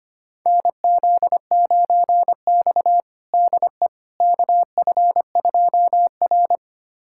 Morse code, heard as a keyed tone, about 25 wpm, 700 Hz. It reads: NZ9X DE KF2R
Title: USA Callsign DE Callsign - 4 Characters - 25wpm